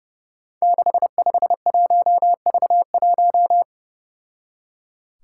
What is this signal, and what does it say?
Morse code, heard as a keyed tone, about 31 wpm, 700 Hz. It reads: 651V1
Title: Five Challenging Letter-Number Combinations 30wpm